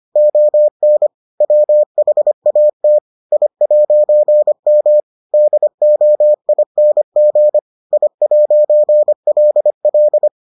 Morse code, heard as a keyed tone, about 25 wpm, 600 Hz. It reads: ON WHAT I'M DOING I'LL